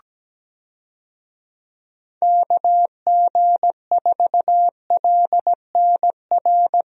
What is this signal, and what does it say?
Morse code, heard as a keyed tone, about 17 wpm, 700 Hz. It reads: KG4LNR
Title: USA Call Signs - 6 Characters - Repeated 3x - 17wpm